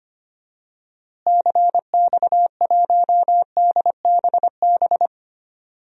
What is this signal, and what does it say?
Morse code, heard as a keyed tone, about 25 wpm, 700 Hz. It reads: CX1DBB
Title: DXCC-100 Call Signs - 6 Characters - Repeated 2x - 25wpm